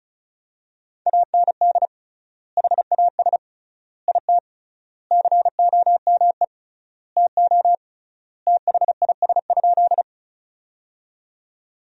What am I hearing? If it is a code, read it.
Morse code, heard as a keyed tone, about 36 wpm, 700 Hz. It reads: AND HAS IT COME TO THIS?
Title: 3x Word-Spacing - Sentences from Top 100 Words - 35wpm